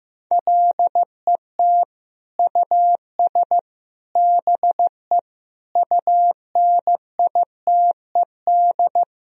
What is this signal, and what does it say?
Morse code, heard as a keyed tone, about 15 wpm, 700 Hz. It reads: LET US BE UNITED
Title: Sentences from Top 1000 Words - Review Part 4 - 15wpm